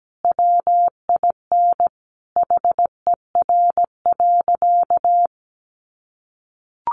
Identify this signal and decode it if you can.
Morse code, heard as a keyed tone, about 17 wpm, 700 Hz. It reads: WIN HER.
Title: Sentences from Top 1500 Words 17wpm